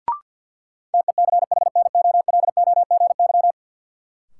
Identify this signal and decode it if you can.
Morse code, heard as a keyed tone, about 53 wpm, 700 Hz. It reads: TECHNOLOGY
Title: Sentences from Top 1200 Words - New Word Drill Alt - 50wpm